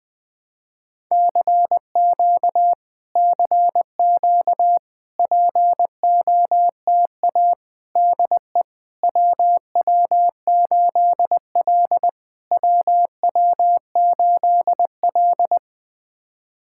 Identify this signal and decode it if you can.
Morse code, heard as a keyed tone, about 20 wpm, 700 Hz. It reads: CQ CQ POTA DE WW8L WW8L
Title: POTA QSO Practice - 20wpm